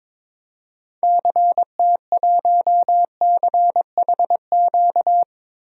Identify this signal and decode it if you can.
Morse code, heard as a keyed tone, about 21 wpm, 700 Hz. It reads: CT1CHQ
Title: DXCC-100 Call Signs - 6 Characters - 22wpm